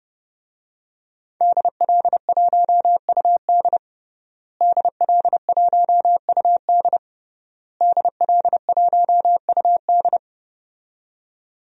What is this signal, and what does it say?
Morse code, heard as a keyed tone, about 30 wpm, 700 Hz. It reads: DL1UD DL1UD DL1UD
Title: DXCC-100 Call Signs - 5 Characters - Repeated 3x - 30wpm